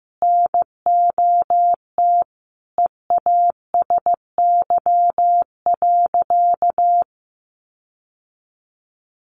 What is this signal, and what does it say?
Morse code, heard as a keyed tone, about 15 wpm, 700 Hz. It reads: NOT EASY.
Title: Sentences from Top 1500 Words 15wpm